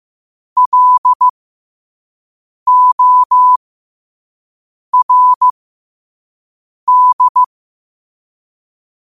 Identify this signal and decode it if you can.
Morse code, heard as a keyed tone, about 15 wpm, spaced as for 5 wpm, 1000 Hz. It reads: LORD
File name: qotdfiles / 5wpm